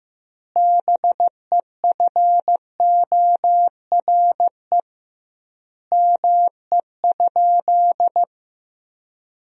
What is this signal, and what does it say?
Morse code, heard as a keyed tone, about 15 wpm, 700 Hz. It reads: BEFORE ME?
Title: 2x Word-Spacing - Sentences from Top 200 Words - 15wpm